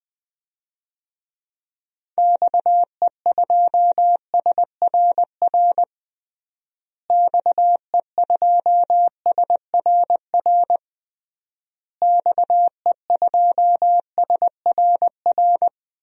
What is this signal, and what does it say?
Morse code, heard as a keyed tone, about 21 wpm, 700 Hz. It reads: XE2SRR XE2SRR XE2SRR
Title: DXCC-100 Call Signs - 6 Characters - Repeated 3x - 20wpm